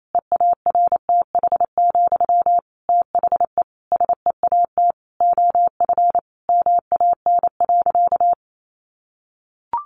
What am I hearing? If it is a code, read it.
Morse code, heard as a keyed tone, about 28 wpm, 700 Hz. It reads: EARTH, THE SEAT OF MAN.
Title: Sentences from Top 1500 Words 28wpm